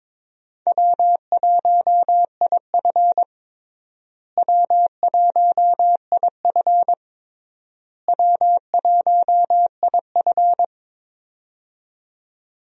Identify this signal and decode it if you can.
Morse code, heard as a keyed tone, about 22 wpm, 700 Hz. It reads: W1IF W1IF W1IF
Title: USA Call Signs - 4 Characters - Repeated 3x - 22wpm